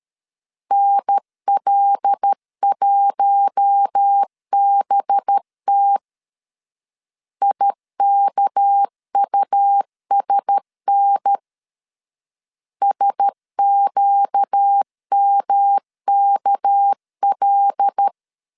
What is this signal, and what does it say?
Morse code, heard as a keyed tone, about 13 wpm, 800 Hz. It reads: NL1BT IKUSN SQMKL
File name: LEZIONI / 6 PKX0 / 6 PK 12wpm 2006